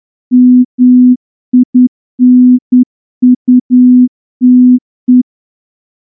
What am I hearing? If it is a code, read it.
Morse code, heard as a keyed tone, about 10 wpm, 250 Hz. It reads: MINUTE
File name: morse-training-10WPM